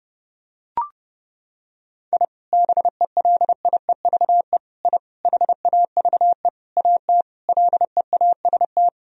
Morse code, heard as a keyed tone, about 30 wpm, 700 Hz. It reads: I BELIEVE I HAVE AT LEAST
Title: Sentences from Top 1000 Words - Review Part 2 - 30wpm